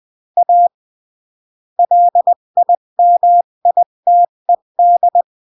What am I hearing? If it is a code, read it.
Morse code, heard as a keyed tone, about 20 wpm, 700 Hz. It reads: A LIMITED
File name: CW / jodorowsky / 82 commandments 20-15